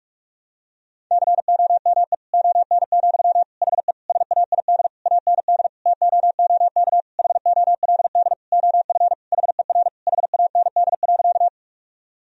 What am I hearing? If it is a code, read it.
Morse code, heard as a keyed tone, about 46 wpm, 700 Hz. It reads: COME ON, HE SAID AND TOOK HOLD OF HER HAND.
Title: Sentences from Top 1000 Words - Review Part 2 - 45wpm